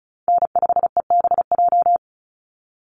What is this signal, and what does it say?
Morse code, heard as a keyed tone, about 34 wpm, 700 Hz. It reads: N5EBJ
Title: USA Call Signs 35wpm